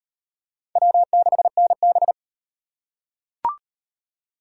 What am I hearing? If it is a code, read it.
Morse code, heard as a keyed tone, about 36 wpm, 700 Hz. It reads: W6NB